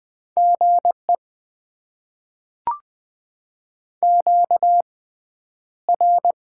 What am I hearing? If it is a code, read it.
Morse code, heard as a keyed tone, about 20 wpm, spaced as for 10 wpm, 700 Hz. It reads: GE QR
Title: Top 20 QSO Elements - 6x Character Spacing - 20wpm